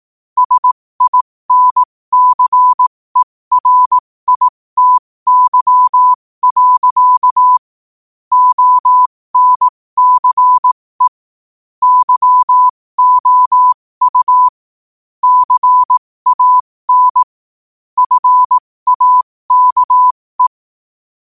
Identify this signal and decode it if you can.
Morse code, heard as a keyed tone, about 18 wpm, spaced as for 15 wpm, 1000 Hz. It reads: SINCERITY. ONCE YOU CAN FAKE
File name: qotdfiles / 15wpm